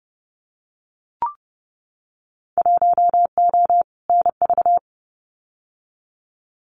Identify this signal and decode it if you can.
Morse code, heard as a keyed tone, about 30 wpm, 700 Hz. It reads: 1O NV
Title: Winter Field Day Exchanges Encore 30wpm